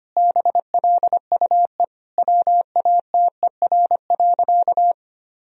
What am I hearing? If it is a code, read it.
Morse code, heard as a keyed tone, about 25 wpm, 700 Hz. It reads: BLUE WATER.
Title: Sentences from Top 1000 Words - Review Part 3 - 25wpm